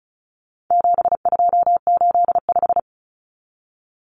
Morse code, heard as a keyed tone, about 34 wpm, 700 Hz. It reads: 7285
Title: Common Cut Numbers - 4-Digit 35wpm